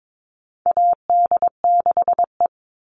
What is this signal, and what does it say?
Morse code, heard as a keyed tone, about 22 wpm, 700 Hz. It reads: AD6E
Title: Call Signs Repeated 3x - 22wpm